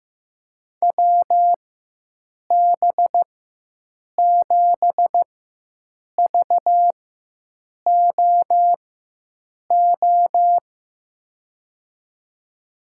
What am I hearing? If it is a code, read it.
Morse code, heard as a keyed tone, about 15 wpm, spaced as for 7 wpm, 700 Hz. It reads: WB7VOO
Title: USA Call Signs - 6 Characters - 4x Character Spacing - 15wpm